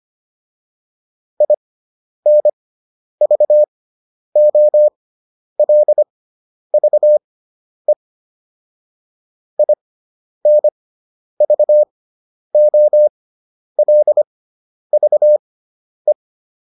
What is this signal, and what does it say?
Morse code, heard as a keyed tone, about 25 wpm, spaced as for 10 wpm, 600 Hz. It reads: INVOLVE INVOLVE